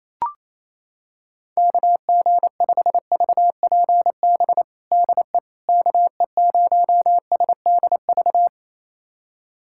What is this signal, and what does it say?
Morse code, heard as a keyed tone, about 28 wpm, 700 Hz. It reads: KG5VPB DE KE0SDV
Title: USA Callsign DE Callsign 28wpm